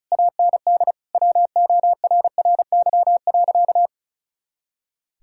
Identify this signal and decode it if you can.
Morse code, heard as a keyed tone, about 36 wpm, 700 Hz. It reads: AND WORRY.
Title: Pangram Challenge v2 - 35wpm